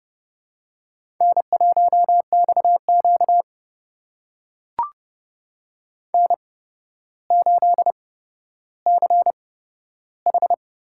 Morse code, heard as a keyed tone, about 30 wpm, spaced as for 13 wpm, 700 Hz. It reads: N1XQ N8CH
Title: USA Call Signs - 4 Characters - 8x Character Spacing - 30wpm